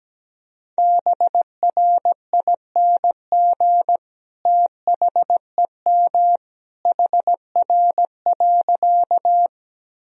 Morse code, heard as a keyed tone, about 17 wpm, 700 Hz. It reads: BRING THEM HR.
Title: Sentences from Top 500 Words and Abbreviations 17wpm